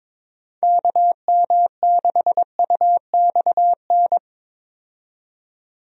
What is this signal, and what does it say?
Morse code, heard as a keyed tone, about 22 wpm, 700 Hz. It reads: KM6UXN
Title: USA Call Signs - 6 Characters - 22wpm